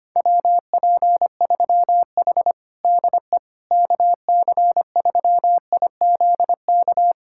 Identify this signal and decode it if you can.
Morse code, heard as a keyed tone, about 25 wpm, 700 Hz. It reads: WP3H DE KC3IZK
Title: USA Callsign DE Callsign 25wpm